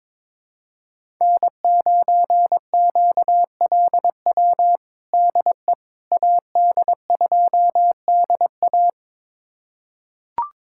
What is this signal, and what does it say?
Morse code, heard as a keyed tone, about 22 wpm, 700 Hz. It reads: N9QLW DE AD2DA
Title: USA Callsign DE Callsign - 5 Characters - 22wpm